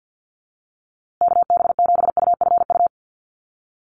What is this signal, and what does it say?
Morse code, heard as a keyed tone, about 50 wpm, 700 Hz. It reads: KB7UFU
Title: USA Call Signs Repeated 3x 50wpm